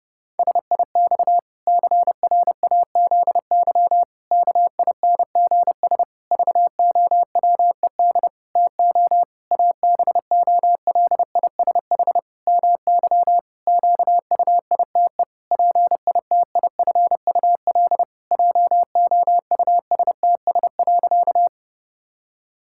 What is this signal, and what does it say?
Morse code, heard as a keyed tone, about 30 wpm, 700 Hz. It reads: SIX CRAZY KINGS VOWED TO ABOLISH MY QUITE PITIFUL JOUSTS.
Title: Pangram Challenge v2 - 30wpm